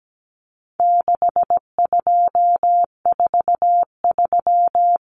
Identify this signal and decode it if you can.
Morse code, heard as a keyed tone, about 17 wpm, 700 Hz. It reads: 6243
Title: Numbers - 4-Digit 17wpm